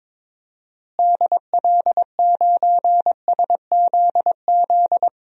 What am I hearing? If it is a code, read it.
Morse code, heard as a keyed tone, about 22 wpm, 700 Hz. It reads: DL9SZZ
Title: DXCC-100 Call Signs - 6 Characters - Repeated 2x - 22wpm